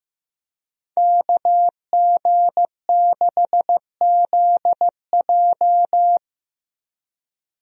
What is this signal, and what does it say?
Morse code, heard as a keyed tone, about 15 wpm, 700 Hz. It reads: KG6ZJ
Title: USA Call Signs - 5 Characters - Repeated 3x - 15wpm